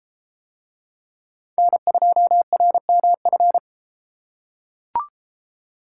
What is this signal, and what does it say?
Morse code, heard as a keyed tone, about 32 wpm, 700 Hz. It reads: N2RMF